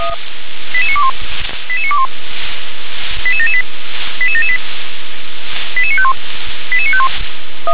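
Five-tone selective call (ZVEI), heard as quick runs of five tones, tones of about 70 ms. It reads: 88921 88921 88778 88778 88951 88951
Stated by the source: ZVEI five-tone selective call received on an SDR.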